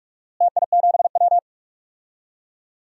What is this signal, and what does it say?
Morse code, heard as a keyed tone, about 46 wpm, 700 Hz. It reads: TI7W
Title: DXCC-100 Call Signs - 4 Characters - Repeated 3x - 45wpm